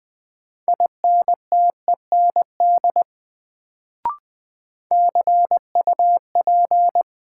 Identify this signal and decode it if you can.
Morse code, heard as a keyed tone, about 20 wpm, 700 Hz. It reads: INTEND CUP
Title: Sentences from Top 1600 Words - New Word Drill - 20wpm